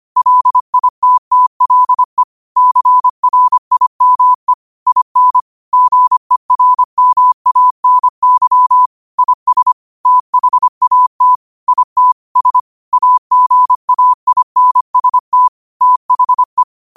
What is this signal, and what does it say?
Morse code, heard as a keyed tone, about 25 wpm, 1000 Hz. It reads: LITTLE CRIME IN GERMANY IS THAT IT S AGAINST THE
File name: qotdfiles / 25wpm